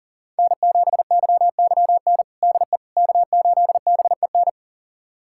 Morse code, heard as a keyed tone, about 40 wpm, 700 Hz. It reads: N7YYN DE K8BEN
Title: USA Callsign DE Callsign - 5 Characters - 40wpm